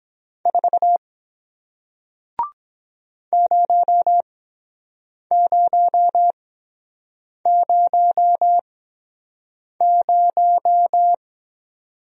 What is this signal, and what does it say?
Morse code, heard as a keyed tone, about 23 wpm, spaced as for 7 wpm, 700 Hz. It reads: 4 0000